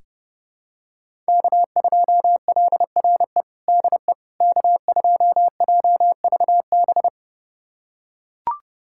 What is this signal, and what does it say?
Morse code, heard as a keyed tone, about 30 wpm, 700 Hz. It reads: K2LRE DE K2JVB
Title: USA Callsign DE Callsign - 5 Characters - 30wpm